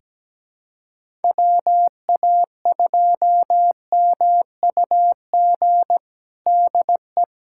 Morse code, heard as a keyed tone, about 17 wpm, 700 Hz. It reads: WA2MUG DE
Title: USA Callsign DE Callsign - 6 Characters - 17wpm